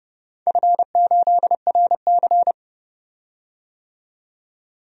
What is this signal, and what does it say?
Morse code, heard as a keyed tone, about 30 wpm, 700 Hz. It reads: F8RC
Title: DXCC-100 Call Signs - 4 Characters - 30wpm